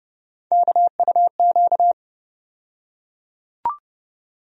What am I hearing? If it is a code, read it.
Morse code, heard as a keyed tone, about 30 wpm, 700 Hz. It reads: KUQ
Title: USA Call Sign Postfixes - 3 Letters 30wpm